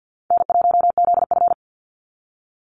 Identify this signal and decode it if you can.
Morse code, heard as a keyed tone, about 50 wpm, 700 Hz. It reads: N1ZF
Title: NA SOTA Activator Callsigns 50wpm